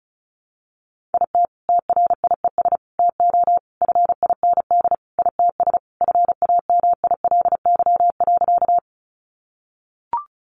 Morse code, heard as a keyed tone, about 34 wpm, 700 Hz. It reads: IT TRIES TO FIND ITS FAMILY.
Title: Additional Easy Phrases 35wpm